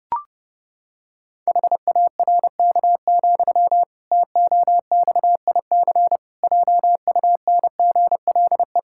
Morse code, heard as a keyed tone, about 30 wpm, 700 Hz. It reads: HARK, TOXIC JUNGLE
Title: Pangram Challenge v2 - 30wpm